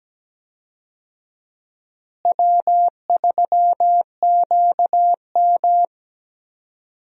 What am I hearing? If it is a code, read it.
Morse code, heard as a keyed tone, about 17 wpm, 700 Hz. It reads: W3QM
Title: USA Call Signs - 4 Characters - Repeated 3x - 17wpm